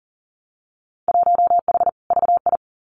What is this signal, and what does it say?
Morse code, heard as a keyed tone, about 41 wpm, 700 Hz. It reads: 1H VI
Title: Winter Field Day Exchanges Encore 40wpm